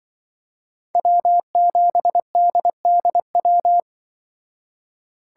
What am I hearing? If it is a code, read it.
Morse code, heard as a keyed tone, about 23 wpm, 700 Hz. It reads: W7DDW
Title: Speed Racing - USA 1x3 Call Signs 20wpm